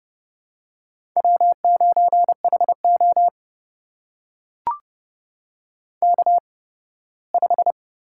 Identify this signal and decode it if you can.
Morse code, heard as a keyed tone, about 30 wpm, spaced as for 18 wpm, 700 Hz. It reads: W9HO K5
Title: USA Call Signs - 4 Characters - 8x Character Spacing - 30wpm